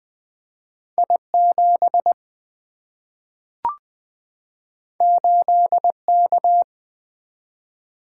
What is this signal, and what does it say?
Morse code, heard as a keyed tone, about 20 wpm, 700 Hz. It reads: I7 8K
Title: Single-Letter Single-Number 20wpm